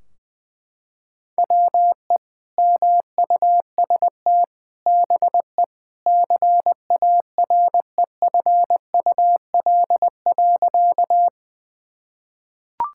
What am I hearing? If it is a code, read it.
Morse code, heard as a keyed tone, about 21 wpm, 700 Hz. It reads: WE MUST BE CAREFUL.
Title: Sentences from Top 1600 Words 20wpm